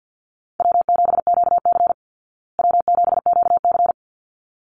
Morse code, heard as a keyed tone, about 48 wpm, 700 Hz. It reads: W7QC W7QC
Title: Easy Call Signs Repeated 2x - 50wpm